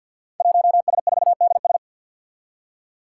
Morse code, heard as a keyed tone, about 49 wpm, 700 Hz. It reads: 1S4DS
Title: Five Challenging Letter-Number Combinations 50wpm